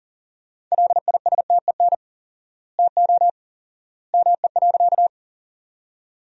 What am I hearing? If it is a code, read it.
Morse code, heard as a keyed tone, about 37 wpm, 700 Hz. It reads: LISTEN TO ME.
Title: 4x Word-Spacing - Sentences from Top 400 Words - 40wpm